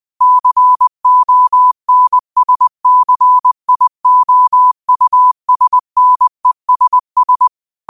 Morse code, heard as a keyed tone, about 20 wpm, 1000 Hz. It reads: CONSCIOUSNESS
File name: qotdfiles / 20wpm